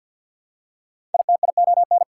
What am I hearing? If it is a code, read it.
Morse code, heard as a keyed tone, about 48 wpm, 690 Hz. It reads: ITION
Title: Prefixes and Suffixes 50wpm